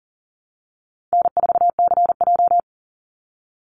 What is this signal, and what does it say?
Morse code, heard as a keyed tone, about 40 wpm, 700 Hz. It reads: N4CJ
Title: Easy Call Signs Repeated 3x - 40wpm